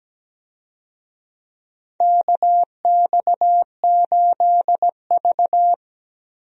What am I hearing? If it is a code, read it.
Morse code, heard as a keyed tone, about 17 wpm, 700 Hz. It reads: KX8V
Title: USA Call Signs - 4 Characters - Repeated 2x - 17wpm